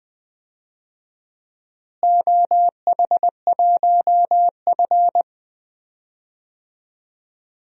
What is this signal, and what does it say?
Morse code, heard as a keyed tone, about 20 wpm, 700 Hz. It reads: OH1F
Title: DXCC-100 Call Signs - 4 Characters - Repeated 3x - 20wpm